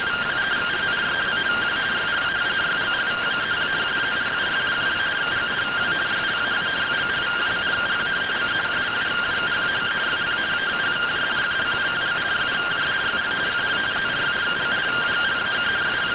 Two-tone teleprinter signal, about 170 Hz shift.